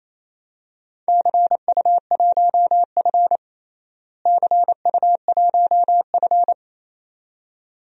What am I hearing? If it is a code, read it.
Morse code, heard as a keyed tone, about 28 wpm, 700 Hz. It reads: CU1F CU1F
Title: DXCC-100 Call Signs - 4 Characters - Repeated 2x - 28wpm